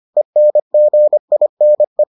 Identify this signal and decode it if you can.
Morse code, heard as a keyed tone, about 25 wpm, 600 Hz. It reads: ENGINE